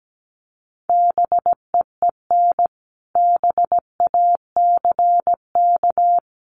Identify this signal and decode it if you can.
Morse code, heard as a keyed tone, about 17 wpm, 700 Hz. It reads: BEEN BACK
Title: Sentences from Top 200 Words 17wpm